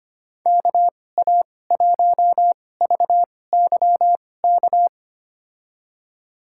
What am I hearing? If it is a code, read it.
Morse code, heard as a keyed tone, about 25 wpm, spaced as for 18 wpm, 700 Hz. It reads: KA1VYK
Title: USA Call Signs - 6 Characters - 2x Character Spacing - 25wpm